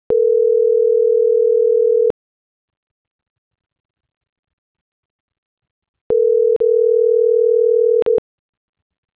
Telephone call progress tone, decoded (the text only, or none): ringback tone